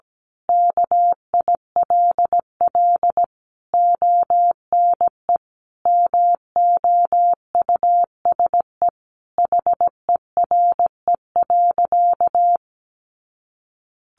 Morse code, heard as a keyed tone, about 17 wpm, 700 Hz. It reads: KILL ONE MOUSE HERE.
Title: A Sound Of Thunder 17wpm